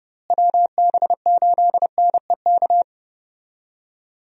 Morse code, heard as a keyed tone, about 31 wpm, 700 Hz. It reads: WB8NEK
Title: USA Call Signs - 6 Characters - Repeated 2x - 30wpm